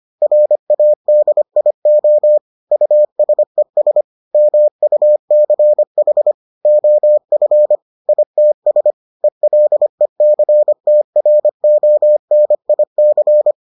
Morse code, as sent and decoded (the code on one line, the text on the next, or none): .-. .- -.. .. --- / ..- ... ... / -- ..- -.-. .... / --- ..-. / .. - ... / . .-.. -.-. - .-. --- -. .. -.-.
RADIO USES MUCH OF ITS ELECTRONIC